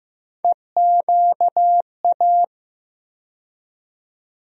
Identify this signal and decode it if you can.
Morse code, heard as a keyed tone, about 15 wpm, 700 Hz. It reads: EQA
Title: USA Call Sign Postfixes - 3 Letters 15wpm